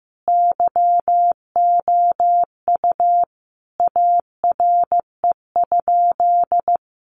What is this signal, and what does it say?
Morse code, heard as a keyed tone, about 14 wpm, 700 Hz. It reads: YOU ARE?
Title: Sentences from Top 700 Words 15wpm